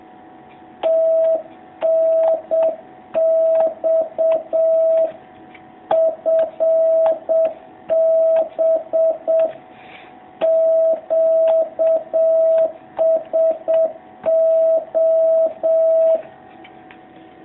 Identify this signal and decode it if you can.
Morse code, heard as a keyed tone, about 7 wpm, 650 Hz. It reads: TNX FB QSO
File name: jh8cbh / 2019cw